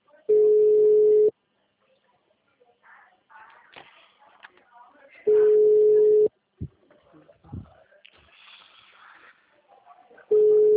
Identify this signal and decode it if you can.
Ringback tone (Europe, 425 Hz), with speech.
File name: out- -2003- - - .92